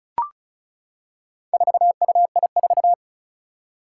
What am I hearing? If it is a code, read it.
Morse code, heard as a keyed tone, about 36 wpm, 700 Hz. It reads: 4UI4
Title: Four Challenging Letter-Number Combinations 35wpm